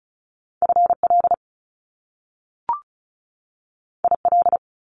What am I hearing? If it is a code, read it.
Morse code, heard as a keyed tone, about 36 wpm, 700 Hz. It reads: FL IL
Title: Top Two Letter Combinations in Words 35wpm